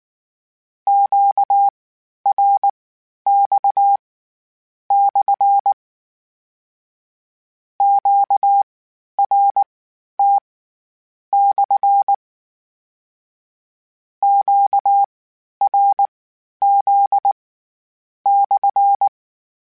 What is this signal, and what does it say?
Morse code, heard as a keyed tone, about 19 wpm, spaced as for 9 wpm, 800 Hz. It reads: QRX/ QRT/ QRZ/